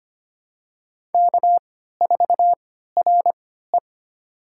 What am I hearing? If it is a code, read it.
Morse code, heard as a keyed tone, about 25 wpm, spaced as for 14 wpm, 700 Hz. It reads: K4RE
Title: USA Call Signs - 4 Characters - 3x Character Spacing - 25wpm